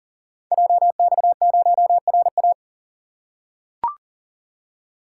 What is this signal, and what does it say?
Morse code, heard as a keyed tone, about 42 wpm, 690 Hz. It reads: JX0RA
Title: Code Groups - 5 Letter:Numbers - 40wpm